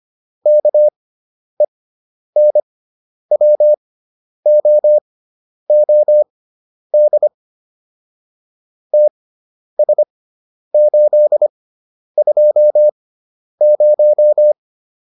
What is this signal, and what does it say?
Morse code, heard as a keyed tone, about 25 wpm, spaced as for 10 wpm, 600 Hz. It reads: KENWOOD TS820